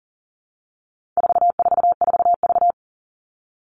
Morse code, heard as a keyed tone, about 40 wpm, 700 Hz. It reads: 444V
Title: All Cut Numbers - 4-Digit 40wpm